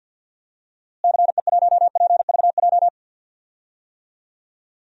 Morse code, heard as a keyed tone, about 52 wpm, 700 Hz. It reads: CE1PVJ
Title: DXCC-100 Call Signs - 6 Characters - Repeated 3x - 50wpm